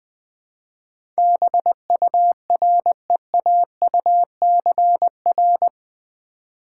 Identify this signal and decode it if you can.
Morse code, heard as a keyed tone, about 20 wpm, 700 Hz. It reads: BUREAUCR
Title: Build A Word - 4 Letters At A Time - 11-Letter Words - 20wpm